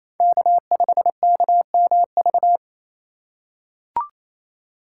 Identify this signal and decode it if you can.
Morse code, heard as a keyed tone, about 28 wpm, 700 Hz. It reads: K5KMV
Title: USA Call Signs - 5 Characters - 4x Character Spacing - 28wpm